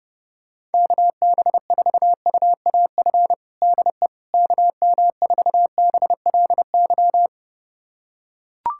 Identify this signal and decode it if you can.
Morse code, heard as a keyed tone, about 32 wpm, 700 Hz. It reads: KB4UAF DE KM4BLY
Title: USA Callsign DE Callsign 30wpm